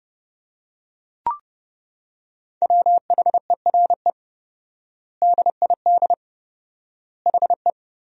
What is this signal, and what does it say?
Morse code, heard as a keyed tone, about 30 wpm, 700 Hz. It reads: WHERE DID HE
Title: 4x Word-Spacing - Sentences from Top 200 Words - 30wpm